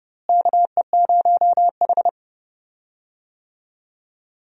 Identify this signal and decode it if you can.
Morse code, heard as a keyed tone, about 30 wpm, 700 Hz. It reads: KE0H
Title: USA Call Signs - 4 Characters - Repeated 2x - 30wpm